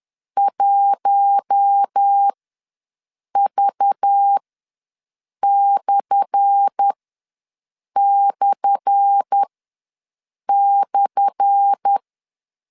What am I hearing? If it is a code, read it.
Morse code, heard as a keyed tone, about 10 wpm, spaced as for 6 wpm, 800 Hz. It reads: 1V///
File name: LEZIONI / 10 SIGNS / 10A SLASH ws 10wpm 4121